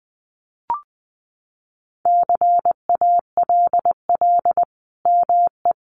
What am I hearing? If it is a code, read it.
Morse code, heard as a keyed tone, about 20 wpm, 700 Hz. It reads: CALL ME
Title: Sentences from Top 200 Words 20wpm